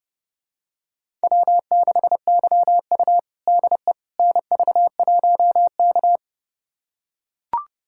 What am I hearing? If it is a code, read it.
Morse code, heard as a keyed tone, about 32 wpm, 700 Hz. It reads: W6YU DE NV1K
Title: USA Callsign DE Callsign - 4 Characters - 30wpm